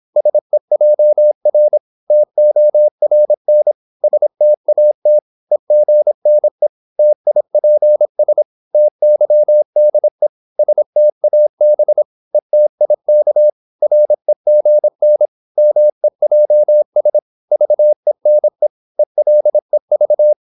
Morse code, heard as a keyed tone, about 26 wpm, 600 Hz. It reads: SEJR TORN STAT EGNE TIPS TYDE STAB ETIK REGN MEJS VENE ELEV
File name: Korte ord DK 26wpm